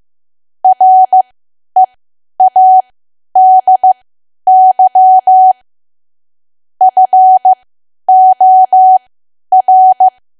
Morse code, heard as a keyed tone, about 15 wpm, spaced as for 10 wpm, 750 Hz. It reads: READY FOR
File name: Archive / 10 WPM / 10WPM